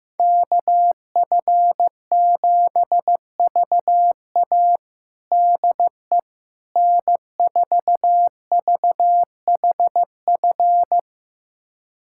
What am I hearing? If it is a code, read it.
Morse code, heard as a keyed tone, about 15 wpm, 700 Hz. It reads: KF7VA DE N4VHF
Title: USA Callsign DE Callsign - 5 Characters - 15wpm